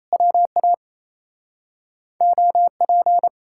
Morse code, heard as a keyed tone, about 38 wpm, 700 Hz. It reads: WA OP